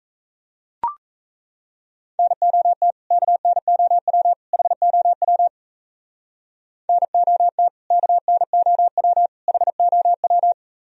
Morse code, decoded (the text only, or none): NOT KNOW HOW NOT KNOW HOW